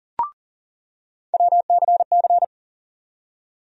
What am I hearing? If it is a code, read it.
Morse code, heard as a keyed tone, about 40 wpm, 700 Hz. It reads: WCC
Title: USA Call Sign Postfixes - 3 Letters 40wpm